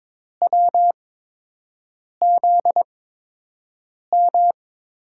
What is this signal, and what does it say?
Morse code, heard as a keyed tone, about 22 wpm, spaced as for 6 wpm, 700 Hz. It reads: WZM
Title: USA Call Signs - 6 Characters - 8x Character Spacing - 22wpm